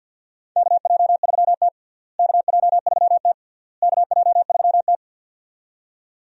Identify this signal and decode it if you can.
Morse code, heard as a keyed tone, about 49 wpm, 700 Hz. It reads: KJ3T KJ3T KJ3T
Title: USA Call Signs - 4 Characters - Repeated 3x - 50wpm